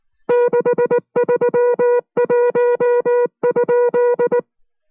Morse code, heard as a keyed tone, about 19 wpm, 480 Hz. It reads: 631?